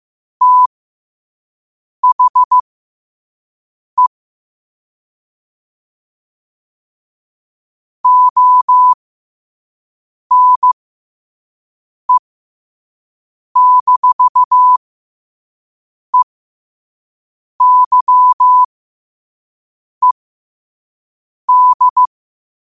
Morse code, heard as a keyed tone, about 15 wpm, spaced as for 5 wpm, 1000 Hz. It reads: THE ONE-EYED